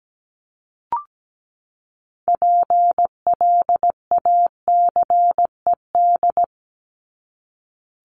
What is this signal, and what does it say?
Morse code, heard as a keyed tone, about 17 wpm, 700 Hz. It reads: PLACED
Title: ED Suffix - Top 100 Words with Base Words 17wpm